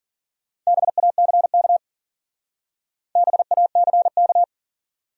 Morse code, heard as a keyed tone, about 52 wpm, 700 Hz. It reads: BACK BACK